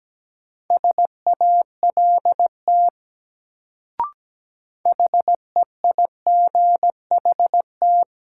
Morse code, heard as a keyed tone, about 17 wpm, 700 Hz. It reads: SALT HEIGHT
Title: Sets of 5 Words - New Word Drill - 17wpm